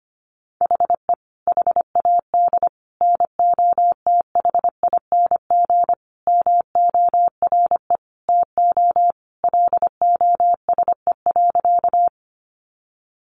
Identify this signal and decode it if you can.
Morse code, heard as a keyed tone, about 25 wpm, 700 Hz. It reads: HE HAD NOTHING MORE TO LOSE.
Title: Sentences from Top 1100 Words 25wpm